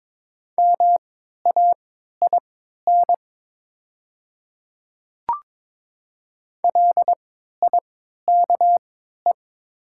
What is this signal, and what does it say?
Morse code, heard as a keyed tone, about 22 wpm, spaced as for 12 wpm, 700 Hz. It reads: MAIN LIKE
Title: 3x Character Spacing - Top 300 Words - 22wpm